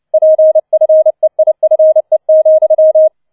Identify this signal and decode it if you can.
Morse code, heard as a keyed tone, about 29 wpm, 630 Hz. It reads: PFEIFE,